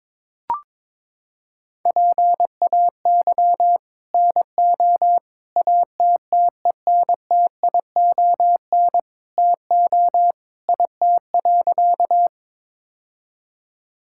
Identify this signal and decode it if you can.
Morse code, heard as a keyed tone, about 22 wpm, 700 Hz. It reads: PAY NO ATTENTION TO IT.
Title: Sentences from Top 1000 Words - Review Part 6 - 22wpm